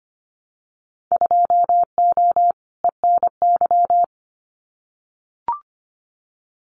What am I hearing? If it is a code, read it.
Morse code, heard as a keyed tone, about 25 wpm, 700 Hz. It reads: 2O ENY
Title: Winter Field Day Exchanges Encore 25wpm